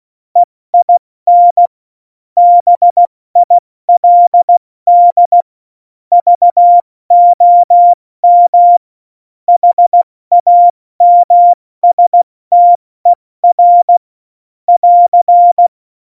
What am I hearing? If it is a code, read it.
Morse code, heard as a keyed tone, about 16 wpm, 700 Hz. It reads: EIN BILD VOM HAMSTER <AR>
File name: Morsekurs / Witze / Witz 17.Juli 2018, 70 F80